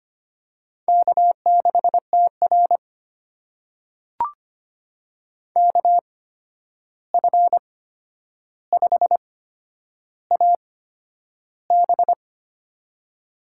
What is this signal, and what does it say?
Morse code, heard as a keyed tone, about 25 wpm, spaced as for 10 wpm, 700 Hz. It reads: K6TR KF5AB
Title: USA Call Signs - 8x Character Spacing - 25wpm